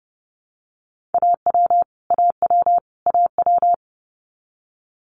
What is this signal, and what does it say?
Morse code, heard as a keyed tone, about 30 wpm, 700 Hz. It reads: AW AW AW
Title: Multi-letter Phonemes Repeated 3x 30wpm